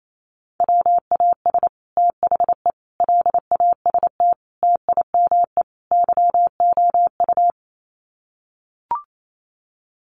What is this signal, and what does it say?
Morse code, heard as a keyed tone, about 28 wpm, 700 Hz. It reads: WAS THE LAST TIME YOU
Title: Sets of 5 Words Encore2 28wpm